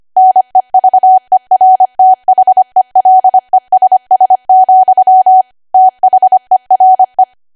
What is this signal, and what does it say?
Morse code, heard as a keyed tone, about 25 wpm, 750 Hz. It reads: NEVERTHELESS, THERE
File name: Archive / 25 WPM / 25WPM